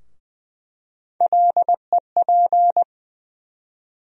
Morse code, heard as a keyed tone, about 20 wpm, 700 Hz. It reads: LEP
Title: USA Call Sign Postfixes - 3 Letters 20wpm